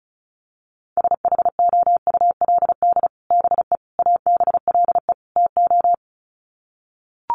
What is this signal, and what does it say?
Morse code, heard as a keyed tone, about 34 wpm, 700 Hz. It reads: SHOULD BE ABLE TO
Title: Sets of 4 Words 35wpm